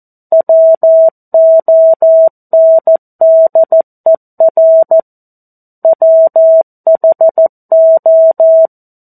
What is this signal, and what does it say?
Morse code, heard as a keyed tone, about 14 wpm, 650 Hz. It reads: WONDER WHO
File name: 02-Chapter-02 14 WPM